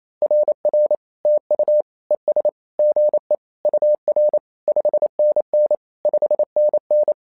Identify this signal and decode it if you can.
Morse code, heard as a keyed tone, about 28 wpm, 600 Hz. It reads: RR TU ES GE UR 5NN 5NN
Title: SOTA QSO Practice 28wpm